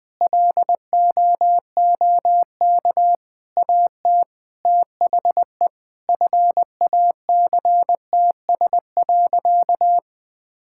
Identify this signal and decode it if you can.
Morse code, heard as a keyed tone, about 20 wpm, 700 Hz. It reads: LOOK AT THE FACTS.
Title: Sentences with 8-Word Phrases - 20wpm